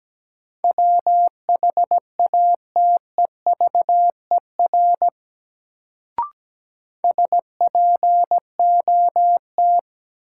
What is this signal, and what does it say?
Morse code, heard as a keyed tone, about 17 wpm, 700 Hz. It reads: WHATEVER SPOT
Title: Sentences from Top 1100 Words - New Word Drill - 17wpm